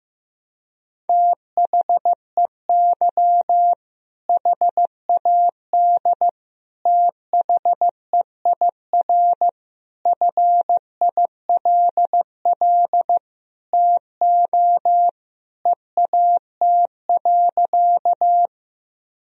Morse code, heard as a keyed tone, about 15 wpm, 700 Hz. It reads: THEY HAD THEIR FILL TO EAT.
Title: Sentences from Top 1000 Words - Review Part 3 - 15wpm